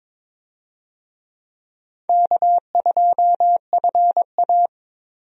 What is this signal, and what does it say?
Morse code, heard as a keyed tone, about 22 wpm, 700 Hz. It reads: K2FA
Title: USA Call Signs - 4 Characters - 22wpm